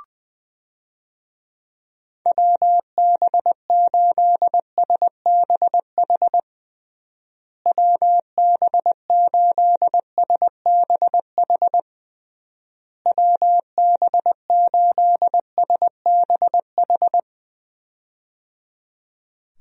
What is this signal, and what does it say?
Morse code, heard as a keyed tone, about 20 wpm, 700 Hz. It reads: WB8SBH WB8SBH WB8SBH
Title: USA Call Signs - 6 Characters - Repeated 3x - 20wpm